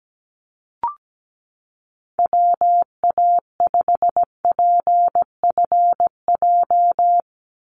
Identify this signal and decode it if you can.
Morse code, heard as a keyed tone, about 17 wpm, 700 Hz. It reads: WA5PFJ
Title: Call Signs - 17wpm